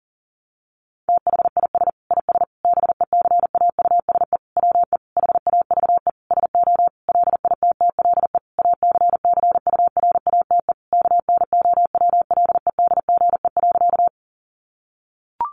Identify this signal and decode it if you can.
Morse code, heard as a keyed tone, about 40 wpm, 700 Hz. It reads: THIS IS BECAUSE WE HAVE SO LITTLE ACCURATE KNOWLEDGE.
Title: Sentences from Top 1200 Words 40wpm